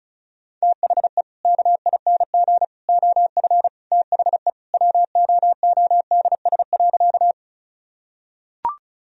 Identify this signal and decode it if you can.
Morse code, heard as a keyed tone, about 36 wpm, 700 Hz. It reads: THE KING OF THE WOODS.
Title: Sentences from Top 1000 Words - Review Part 2 - 35wpm